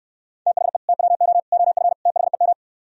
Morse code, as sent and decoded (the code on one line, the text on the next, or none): - .... / - .-- --- / --- ..-. / - .... --
THE TWO OF THEM